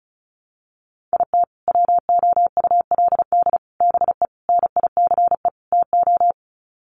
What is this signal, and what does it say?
Morse code, heard as a keyed tone, about 34 wpm, 700 Hz. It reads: IT WOULD BE NICE TO
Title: Sets of 5 Words Encore2 35wpm